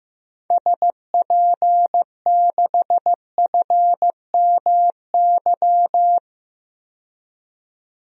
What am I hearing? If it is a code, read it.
Morse code, heard as a keyed tone, about 15 wpm, 700 Hz. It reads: SP6FMY
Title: DXCC-100 Call Signs - 6 Characters - 15wpm